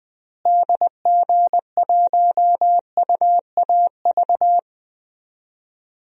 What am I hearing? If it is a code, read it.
Morse code, heard as a keyed tone, about 20 wpm, 700 Hz. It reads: DG1UAV
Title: DXCC-100 Call Signs - 6 Characters - Repeated 2x - 20wpm